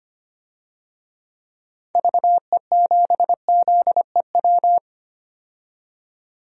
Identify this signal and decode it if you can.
Morse code, heard as a keyed tone, about 25 wpm, 700 Hz. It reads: VE7ZEW
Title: DXCC-100 Call Signs - 6 Characters - 25wpm